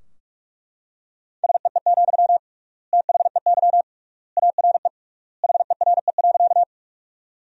Morse code, heard as a keyed tone, about 41 wpm, 700 Hz. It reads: SEE, THEY ARE HERE.
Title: 3x Word-Spacing - Sentences from Top 200 Words - 45wpm